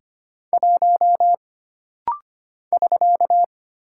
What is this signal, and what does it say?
Morse code, heard as a keyed tone, about 25 wpm, 700 Hz. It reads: 1 <SK>
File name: Single Character (Letter:Number:Punctuation:Prosign) - Rapid-Fire - 25wpm